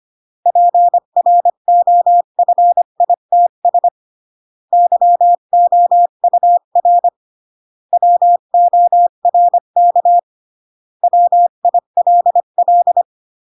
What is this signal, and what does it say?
Morse code, heard as a keyed tone, about 25 wpm, 700 Hz. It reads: PROFITS YOUR WORK WILL
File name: CW / jodorowsky / 82 commandments 25-20